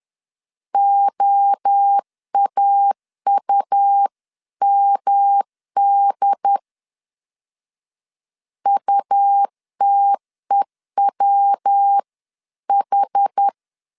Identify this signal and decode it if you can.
Morse code, heard as a keyed tone, about 10 wpm, 800 Hz. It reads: OAUMD UTEWH
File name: CW / DET / 1 3 1500C S C 10wpm 3335